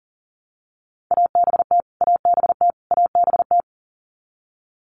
Morse code, heard as a keyed tone, about 40 wpm, 700 Hz. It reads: ABT ABT ABT
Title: Abbreviations Repeated 3x - 40wpm